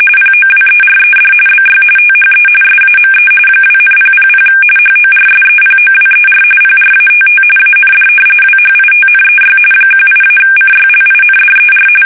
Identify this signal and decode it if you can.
RTTY, about 45.45 baud. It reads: POWERED BY THE SAME 2,000 HORSEPOWER PRATT AND WHITNEY 18-CYLINDER
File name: fsk850